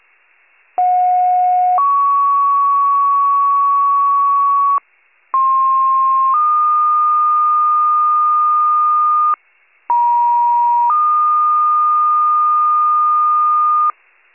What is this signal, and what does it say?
Two-tone page, heard as a one-second tone, then a three-second tone, about 730 then 1120 Hz, about 1060 then 1220 Hz, about 950 then 1220 Hz. It is heard from 0.8 to 13.9 s.
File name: pager / 56 RESCUE / wcalert